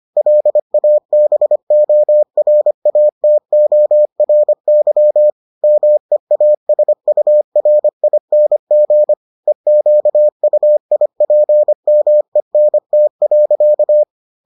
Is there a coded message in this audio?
Yes: Morse code from 0.2 to 14.0 s, heard as a keyed tone, about 25 wpm, 600 Hz. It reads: LABORATORY MEASURING EQUIPMENT.